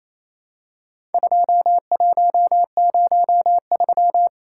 Morse code, heard as a keyed tone, about 28 wpm, 700 Hz. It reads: 2103
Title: Worked All Europe DX Contest - QTC Exchange - 28wpm